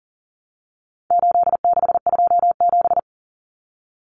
Morse code, heard as a keyed tone, about 39 wpm, 700 Hz. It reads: 8627
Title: Numbers - 4-Digit 40wpm